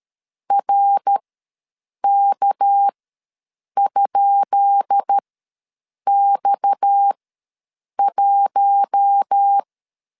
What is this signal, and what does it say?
Morse code, heard as a keyed tone, about 13 wpm, spaced as for 7 wpm, 800 Hz. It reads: RK?X1
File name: LEZIONI / 10 SIGNS / 10B INT ws 12wpm 3527